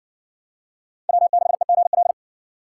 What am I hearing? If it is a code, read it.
Morse code, heard as a keyed tone, about 54 wpm, 700 Hz. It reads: W6EGL